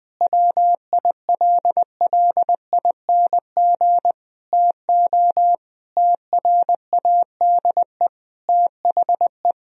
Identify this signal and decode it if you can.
Morse code, heard as a keyed tone, about 20 wpm, 700 Hz. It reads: WILLING TO TRADE THE
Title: Sentences from Top 1000 Words - Review Part 4 - 20wpm